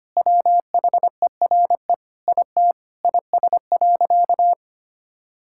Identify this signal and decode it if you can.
Morse code, heard as a keyed tone, about 25 wpm, 700 Hz. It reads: WHERE IT IS.
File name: practice / 25wpm / Sentences with 4-Word Phrases - 25wpm